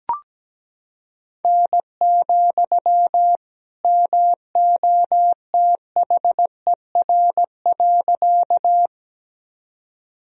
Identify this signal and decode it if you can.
Morse code, heard as a keyed tone, about 17 wpm, 700 Hz. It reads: N, MOTHER.
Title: Sentences from Top 200 Words and Abbreviations 17wpm